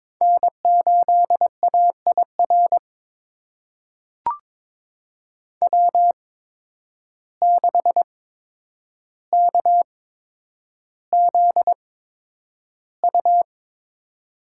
Morse code, heard as a keyed tone, about 22 wpm, spaced as for 9 wpm, 700 Hz. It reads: N8AIR W6KZU
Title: USA Call Signs - 5 Characters - 8x Character Spacing - 22wpm